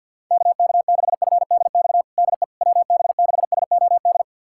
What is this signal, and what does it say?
Morse code, heard as a keyed tone, about 46 wpm, 700 Hz. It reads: KK6FDX DE WB6SOD
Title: USA Callsign DE Callsign - 6 Characters - 50wpm